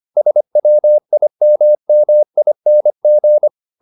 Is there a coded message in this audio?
Yes: Morse code from 0.2 to 3.5 s, heard as a keyed tone, about 25 wpm, 600 Hz. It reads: SWIMMING